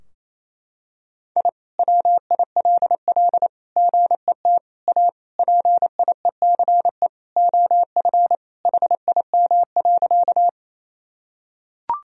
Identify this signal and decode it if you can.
Morse code, heard as a keyed tone, about 27 wpm, 700 Hz. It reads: I WILL GET A PIECE OF HIM.
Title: Sentences from Top 1000 Words - Review Part 2 - 28wpm